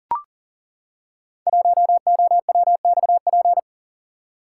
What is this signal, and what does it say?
Morse code, heard as a keyed tone, about 42 wpm, 700 Hz. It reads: 1OWXP
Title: Code Groups - 5 Letter:Numbers - 40wpm